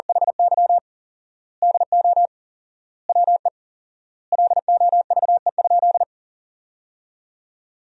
Morse code, heard as a keyed tone, about 40 wpm, 700 Hz. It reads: HY DO WE LOVE?
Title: 4x Word-Spacing - Sentences from Top 400 Words - 40wpm